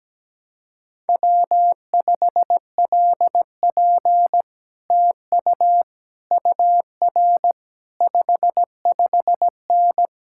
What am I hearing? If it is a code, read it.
Morse code, heard as a keyed tone, about 17 wpm, 700 Hz. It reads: W5LP TU UR 55N
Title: Speed-Racing Common QSO Phrases - 17wpm